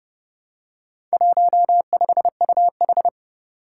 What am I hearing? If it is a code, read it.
Morse code, heard as a keyed tone, about 30 wpm, 700 Hz. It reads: 15UH
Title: Four Challenging Letter-Number Combinations 30wpm